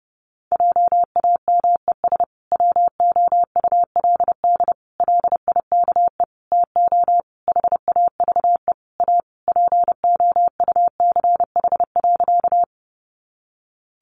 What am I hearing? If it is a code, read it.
Morse code, heard as a keyed tone, about 30 wpm, 700 Hz. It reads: JAMES WOULD LIKE TO HAVE A POUCH.
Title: Easy Phrases 30wpm